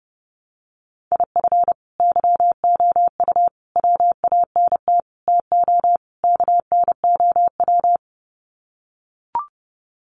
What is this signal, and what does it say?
Morse code, heard as a keyed tone, about 30 wpm, 700 Hz. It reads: IF YOU WANT TO KNOW
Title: Sets of 5 Words Encore2 30wpm